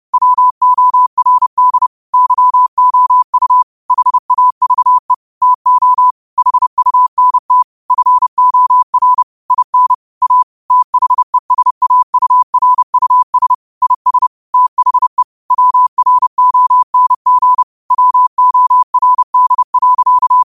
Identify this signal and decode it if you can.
Morse code, heard as a keyed tone, about 30 wpm, 1000 Hz. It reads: WORD YOU HAVE TO HUNT FOR IN A THESAURUS IS THE WRONG WORD.
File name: qotdfiles / 30wpm